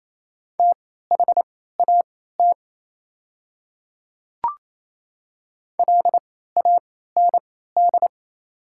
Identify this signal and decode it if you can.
Morse code, heard as a keyed tone, about 29 wpm, spaced as for 16 wpm, 690 Hz. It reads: THAT LAND
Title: 3x Character Spacing - Top 200 Words - 28wpm